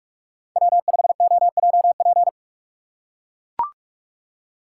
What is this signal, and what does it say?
Morse code, heard as a keyed tone, about 42 wpm, 700 Hz. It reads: W5OJP
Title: USA Call Signs - 5 Characters - 3x Character Spacing - 45wpm